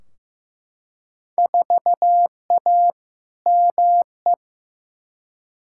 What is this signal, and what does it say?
Morse code, heard as a keyed tone, about 15 wpm, 700 Hz. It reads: 4A ME
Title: Field Day Exchanges - 15wpm